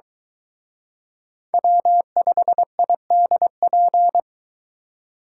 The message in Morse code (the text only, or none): W5IDP